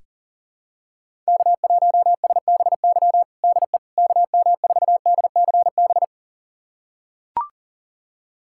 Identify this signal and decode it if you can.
Morse code, heard as a keyed tone, about 40 wpm, 700 Hz. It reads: K1SBY DE KM4DCB
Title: USA Callsign DE Callsign 40wpm